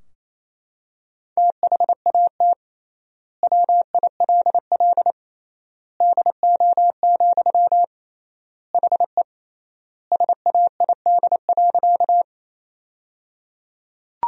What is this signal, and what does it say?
Morse code, heard as a keyed tone, about 28 wpm, 700 Hz. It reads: THAT WILL DO, HE SAID.
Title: 3x Word-Spacing - Sentences from Top 100 Words - 28wpm